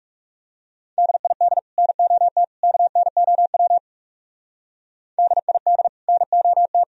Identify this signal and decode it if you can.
Morse code, heard as a keyed tone, about 38 wpm, 700 Hz. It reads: DID NOT KNOW DID NOT